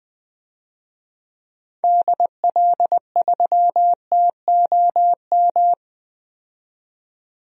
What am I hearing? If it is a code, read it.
Morse code, heard as a keyed tone, about 21 wpm, 700 Hz. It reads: DL3TOM
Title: DXCC-100 Call Signs - 6 Characters - 20wpm